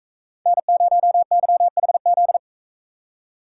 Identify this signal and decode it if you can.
Morse code, heard as a keyed tone, about 44 wpm, 700 Hz. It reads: N0YHZ